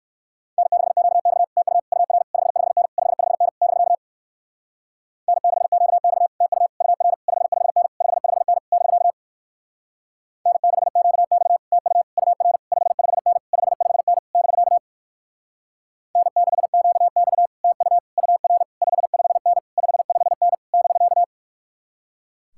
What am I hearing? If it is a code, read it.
Morse code, heard as a keyed tone, about 60 wpm, 700 Hz. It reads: N6QX TU UR 55N 55N <BK> N6QX TU UR 55N 55N <BK> N6QX TU UR 55N 55N <BK> N6QX TU UR 55N 55N <BK>